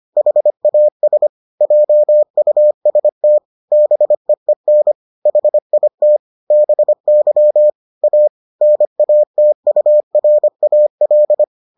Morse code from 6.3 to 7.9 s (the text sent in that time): BY